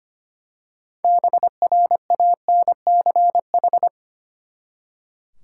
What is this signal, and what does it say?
Morse code, heard as a keyed tone, about 25 wpm, 700 Hz. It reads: BRANCH
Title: Six Letter Words - 25wpm